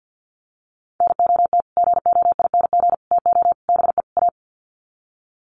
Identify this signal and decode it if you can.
Morse code, heard as a keyed tone, about 50 wpm, 700 Hz. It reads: NOT GOING TO BE A
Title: Sets of 5 Words Encore 50wpm